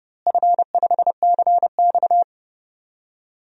Morse code, heard as a keyed tone, about 30 wpm, 700 Hz. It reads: F5CX
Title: DXCC-100 Call Signs - 4 Characters - Repeated 2x - 30wpm